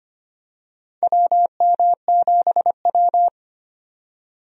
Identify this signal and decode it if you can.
Morse code, heard as a keyed tone, about 25 wpm, 700 Hz. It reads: WM7W
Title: USA Call Signs - 4 Characters - 25wpm